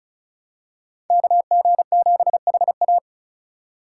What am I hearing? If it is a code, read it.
Morse code, heard as a keyed tone, about 36 wpm, 700 Hz. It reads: KG7HA
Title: USA Call Signs - 5 Characters - 3x Character Spacing - 35wpm